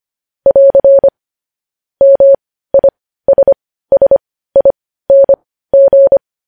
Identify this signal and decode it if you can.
Morse code, heard as a keyed tone, about 25 wpm, spaced as for 15 wpm, 550 Hz. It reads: <AR> MISSING
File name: podcast / News headlines-1x-25x15